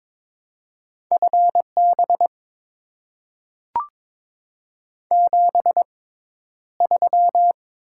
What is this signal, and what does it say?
Morse code, heard as a keyed tone, about 22 wpm, spaced as for 11 wpm, 690 Hz. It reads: FB 73
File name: Top 20 QSO Elements - 6x Character Spacing - 22wpm